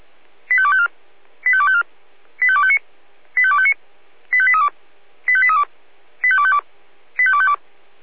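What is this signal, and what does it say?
Five-tone selective call (ZVEI), heard as quick runs of five tones, tones of about 70 ms. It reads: 85254 85254 85259 85259 85832 85832 85252 85252
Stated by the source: ZVEI five-tone selective call 85280 received on an SDR.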